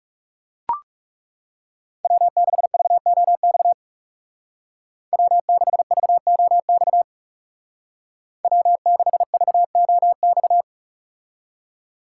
Morse code, heard as a keyed tone, about 38 wpm, 700 Hz. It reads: W6VOX W6VOX W6VOX